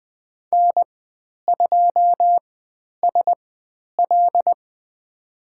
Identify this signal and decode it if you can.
Morse code, heard as a keyed tone, about 20 wpm, spaced as for 10 wpm, 700 Hz. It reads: N2SL
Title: 095 Lesson 16g - Letter c - Call Signs 20wpm